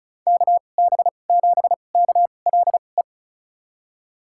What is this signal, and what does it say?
Morse code, heard as a keyed tone, about 34 wpm, spaced as for 25 wpm, 700 Hz. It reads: KB7KLE
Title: USA Call Signs - 6 Characters - 2x Character Spacing - 35wpm